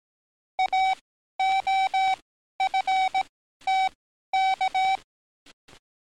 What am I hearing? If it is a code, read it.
Morse code, heard as a keyed tone, about 18 wpm, spaced as for 12 wpm, 760 Hz. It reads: AOFTK
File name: training / NAVPERS 11049to11060 / NAVPERS 11052 RA RB